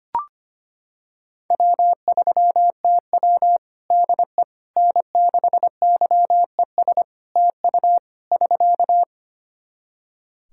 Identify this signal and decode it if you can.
Morse code, heard as a keyed tone, about 25 wpm, 700 Hz. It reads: W3TW DE N6YES TU <SK>
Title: Practice QSO - Exchange 4 - Ending - 25wpm